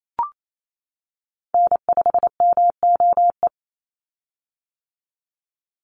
Morse code, heard as a keyed tone, about 28 wpm, 700 Hz. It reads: N5MOE
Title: USA Call Signs 28wpm